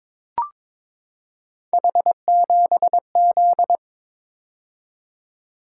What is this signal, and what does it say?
Morse code, heard as a keyed tone, about 22 wpm, 700 Hz. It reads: H7Z
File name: practice / 22wpm / Three Letter-Number Combinations - 22wpm